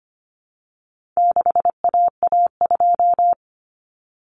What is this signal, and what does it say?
Morse code, heard as a keyed tone, about 25 wpm, 700 Hz. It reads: 6AA2
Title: Common Cut Numbers - 4-Digit 25wpm